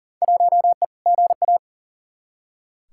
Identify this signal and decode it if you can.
Morse code, heard as a keyed tone, about 40 wpm, 700 Hz. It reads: 1E GA
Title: Field Day Exchanges - 40wpm